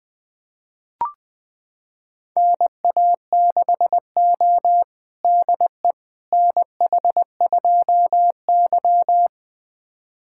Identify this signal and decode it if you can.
Morse code, heard as a keyed tone, about 20 wpm, 700 Hz. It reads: NA6O DE NH2Y
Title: USA Callsign DE Callsign - 4 Characters - 20wpm